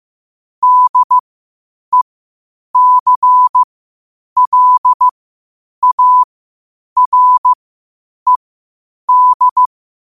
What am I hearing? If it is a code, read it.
Morse code, heard as a keyed tone, about 15 wpm, spaced as for 8 wpm, 1000 Hz. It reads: DECLARED
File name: qotdfiles / 8wpm